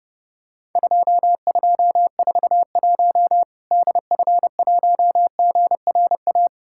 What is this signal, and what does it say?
Morse code, heard as a keyed tone, about 30 wpm, 700 Hz. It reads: 2241 DF1GRA
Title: Worked All Europe DX Contest - QTC Exchange - 30wpm